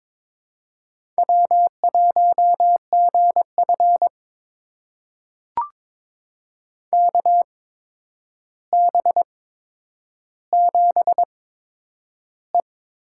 Morse code, heard as a keyed tone, about 22 wpm, spaced as for 9 wpm, 700 Hz. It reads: W1GF KB7E
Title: USA Call Signs - 8x Character Spacing - 22wpm